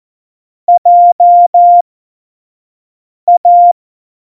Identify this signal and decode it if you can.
Morse code, heard as a keyed tone, about 14 wpm, spaced as for 5 wpm, 700 Hz. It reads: JA
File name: Morsekurs / Witze / Witz 02 Juni 2020, 25 F70